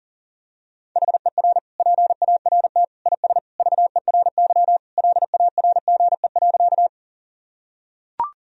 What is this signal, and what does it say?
Morse code, heard as a keyed tone, about 40 wpm, 700 Hz. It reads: HER PART IS VERY LARGE.
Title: Sentences from Top 1000 Words - Review Part 1 - 40wpm